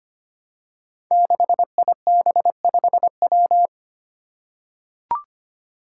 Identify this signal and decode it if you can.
Morse code, heard as a keyed tone, about 25 wpm, 700 Hz. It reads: 6IB5W
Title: Five Challenging Letter-Number Combinations 25wpm